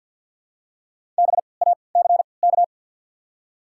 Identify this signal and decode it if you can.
Morse code, heard as a keyed tone, about 48 wpm, 700 Hz. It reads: B A C K
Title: 3x Character Spacing - Top 300 Words - 50wpm